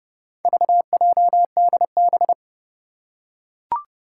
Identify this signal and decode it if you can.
Morse code, heard as a keyed tone, about 30 wpm, 700 Hz. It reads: VJDB
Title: Four Challenging Letter-Number Combinations 30wpm